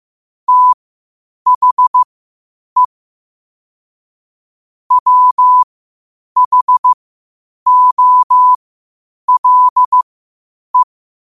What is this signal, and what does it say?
Morse code, heard as a keyed tone, about 15 wpm, spaced as for 8 wpm, 1000 Hz. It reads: THE WHOLE
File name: qotdfiles / 8wpm